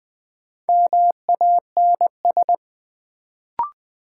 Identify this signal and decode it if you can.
Morse code, heard as a keyed tone, about 20 wpm, 700 Hz. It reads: MANS
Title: Sentences from Top 1000 Words - Review Part 6 - New Word Drill - 20wpm